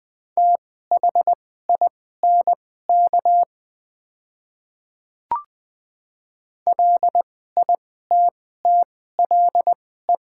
Morse code, heard as a keyed tone, about 20 wpm, spaced as for 14 wpm, 700 Hz. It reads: THINK LITTLE
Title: 2x Character Spacing - Top 300 Words - 20wpm